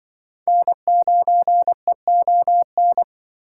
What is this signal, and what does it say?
Morse code, heard as a keyed tone, about 24 wpm, 700 Hz. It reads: N9EON
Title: Speed Racing - USA 1x3 Call Signs 20wpm